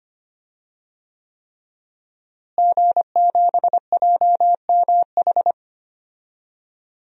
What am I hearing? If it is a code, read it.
Morse code, heard as a keyed tone, about 25 wpm, 700 Hz. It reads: G7JMH
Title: DXCC-100 Call Signs - 5 Characters - Repeated 3x - 25wpm